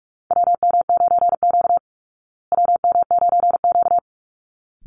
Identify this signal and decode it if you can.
Morse code, heard as a keyed tone, about 44 wpm, 700 Hz. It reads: WM9Q WM9Q
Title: Call Signs Repeated 3x - 45wpm